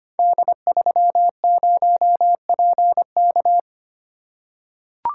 Morse code, heard as a keyed tone, about 25 wpm, 700 Hz. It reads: D30PK
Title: Code Groups - 5 Letter:Numbers - 25wpm